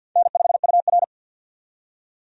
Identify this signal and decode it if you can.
Morse code, heard as a keyed tone, about 53 wpm, 700 Hz. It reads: N5UR
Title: USA Call Signs - 4 Characters - 50wpm